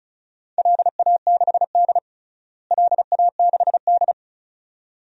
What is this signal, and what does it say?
Morse code, heard as a keyed tone, about 36 wpm, 700 Hz. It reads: LA6D LA6D
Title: DXCC-100 Call Signs - 4 Characters - Repeated 2x - 35wpm